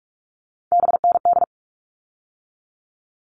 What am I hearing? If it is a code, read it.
Morse code, heard as a keyed tone, about 44 wpm, 700 Hz. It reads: BND
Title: All Cut Numbers - 3-Digit 45wpm